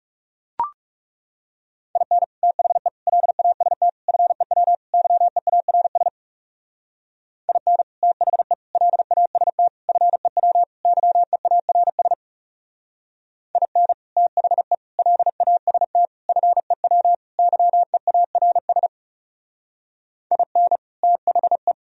Morse code, heard as a keyed tone, about 37 wpm, 700 Hz. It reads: IN THE LAST FEW YEARS IN THE LAST FEW YEARS IN THE LAST FEW YEARS IN THE
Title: Speed-Racing - Sets of 5 Words 30wpm